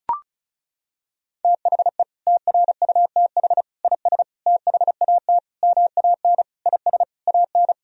Morse code, heard as a keyed tone, about 36 wpm, 700 Hz. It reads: THE TRUTH IS THAT MAN IS AN
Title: Sentences from Top 1000 Words - Review Part 1 - 35wpm